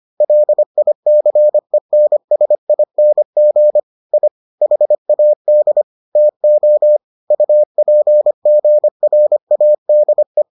Morse code, heard as a keyed tone, about 25 wpm, 600 Hz. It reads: LICENSING I HAD TO UPGRADE